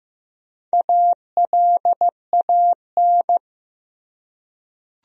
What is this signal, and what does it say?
Morse code, heard as a keyed tone, about 15 wpm, 700 Hz. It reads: ALAN
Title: USA Common Names - Max 4 Letters - 15wpm